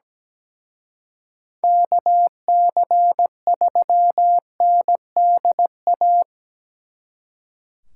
Morse code, heard as a keyed tone, about 19 wpm, 700 Hz. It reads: KC3NDA
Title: Speed Racing - USA 2x3 Call Signs 17wpm